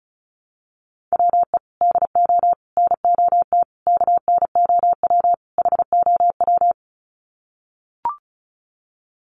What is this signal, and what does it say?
Morse code, heard as a keyed tone, about 34 wpm, 700 Hz. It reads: WE DO NOT KNOW HOW
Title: Sets of 5 Words Encore 35wpm